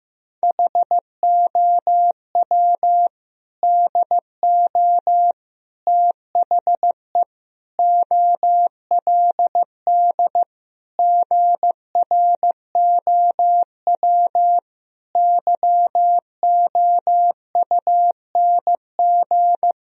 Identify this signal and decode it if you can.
Morse code, heard as a keyed tone, about 15 wpm, 700 Hz. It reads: HOW DO THE OLD GROW YOUNG
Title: Sentences from Top 1000 Words - Review Part 1 - 15wpm